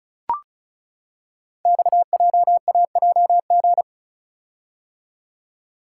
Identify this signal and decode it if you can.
Morse code, heard as a keyed tone, about 36 wpm, 700 Hz. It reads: XJAJG
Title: Code Groups - 5 Letters - 35wpm